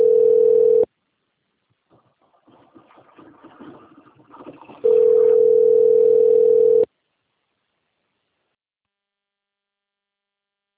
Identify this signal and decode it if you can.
Ringback tone (North America), 440 + 480 Hz.